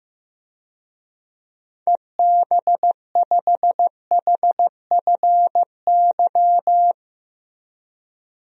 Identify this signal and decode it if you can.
Morse code, heard as a keyed tone, about 16 wpm, 700 Hz. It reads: EB5HFY
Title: DXCC-100 Call Signs - 6 Characters - Repeated 2x - 15wpm